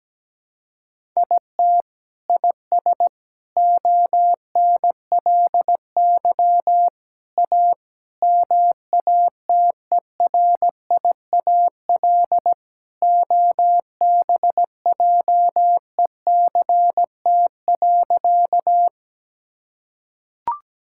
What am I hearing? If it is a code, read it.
Morse code, heard as a keyed tone, about 17 wpm, 700 Hz. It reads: IT IS ONLY A MATERIAL OBJECT.
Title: Sentences from Top 1000 Words - Review Part 5 - 17wpm